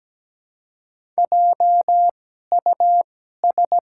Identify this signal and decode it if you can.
Morse code, heard as a keyed tone, about 17 wpm, spaced as for 12 wpm, 700 Hz. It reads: JUS
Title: 2x Character Spacing - Top 200 Words - 17wpm